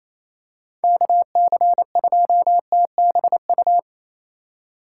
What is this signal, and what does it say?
Morse code, heard as a keyed tone, about 28 wpm, 700 Hz. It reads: KC2TBU
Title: USA Call Signs - 6 Characters - 3x Character Spacing - 28wpm